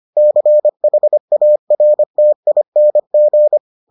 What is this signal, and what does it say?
Morse code, heard as a keyed tone, about 25 wpm, 600 Hz. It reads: CHARTING